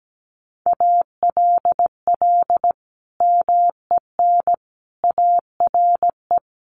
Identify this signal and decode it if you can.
Morse code, heard as a keyed tone, about 17 wpm, 700 Hz. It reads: ALL MEN ARE
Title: Sentences from Top 1500 Words 17wpm